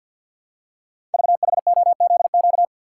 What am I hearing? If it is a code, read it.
Morse code, heard as a keyed tone, about 50 wpm, 690 Hz. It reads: VHOZQ
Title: Code Groups - 5 Letters - 50wpm